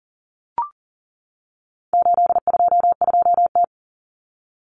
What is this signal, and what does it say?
Morse code, heard as a keyed tone, about 40 wpm, 700 Hz. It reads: 822T
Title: Common Cut Numbers - 4-Digit 40wpm